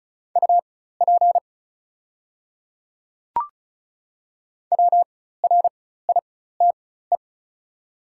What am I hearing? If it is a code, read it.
Morse code, heard as a keyed tone, about 36 wpm, spaced as for 16 wpm, 700 Hz. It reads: UP WRITE
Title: 4x Character Spacing - Top 100 Words - 35wpm